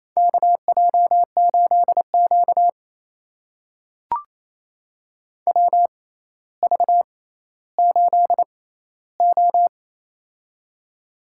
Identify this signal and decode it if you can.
Morse code, heard as a keyed tone, about 28 wpm, spaced as for 14 wpm, 700 Hz. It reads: KJ8Q WV8O
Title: USA Call Signs - 4 Characters - 6x Character Spacing - 28wpm